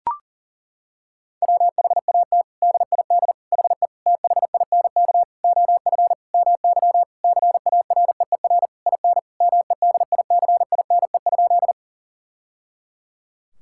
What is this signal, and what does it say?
Morse code, heard as a keyed tone, about 37 wpm, 700 Hz. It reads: WHAT DID HE THINK OF MY CAREER IN MEDICINE?
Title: Sentences from Top 1000 Words - Review Part 6 - 40wpm